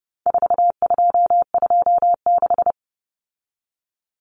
Morse code, heard as a keyed tone, about 30 wpm, 700 Hz. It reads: 4226
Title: Common Cut Numbers - 4-Digit 30wpm